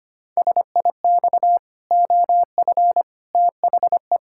Morse code, heard as a keyed tone, about 25 wpm, 700 Hz. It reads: SIX OF THE
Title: Pangram Challenge v2 - 25wpm